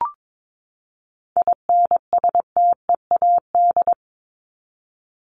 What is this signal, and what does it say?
Morse code, heard as a keyed tone, about 22 wpm, 700 Hz. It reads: INSTEAD
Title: Top 501-750 Words 22wpm